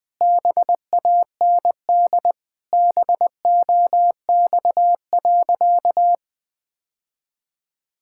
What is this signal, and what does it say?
Morse code, heard as a keyed tone, about 21 wpm, 700 Hz. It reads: BAND BOX.
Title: Sentences from Top 1600 Words 20wpm